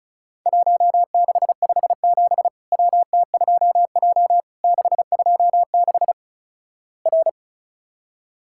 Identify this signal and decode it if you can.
Morse code, heard as a keyed tone, about 36 wpm, 700 Hz. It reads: 1657 WT2J 626 R
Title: Worked All Europe DX Contest - QTC Exchange - 35wpm